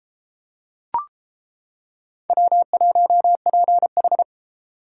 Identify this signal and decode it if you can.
Morse code, heard as a keyed tone, about 31 wpm, 700 Hz. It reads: W1PH